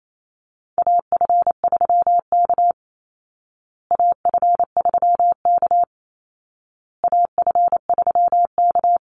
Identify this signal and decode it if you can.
Morse code, heard as a keyed tone, about 28 wpm, 700 Hz. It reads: AF3K AF3K AF3K
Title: Call Signs Repeated 2x - 28wpm